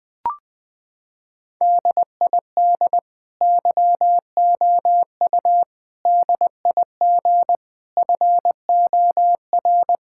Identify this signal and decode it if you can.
Morse code, heard as a keyed tone, about 21 wpm, 700 Hz. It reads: DID YOU DIG FOR
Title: Sentences from Top 1600 Words 20wpm